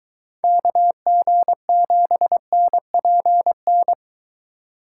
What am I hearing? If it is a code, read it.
Morse code, heard as a keyed tone, about 23 wpm, 700 Hz. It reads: KG7NPN